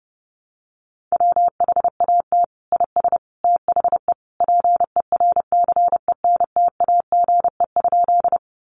Morse code, heard as a keyed tone, about 30 wpm, 700 Hz. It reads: WHAT IS THE PERCENTAGE?
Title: Sentences from Top 1300 Words 30wpm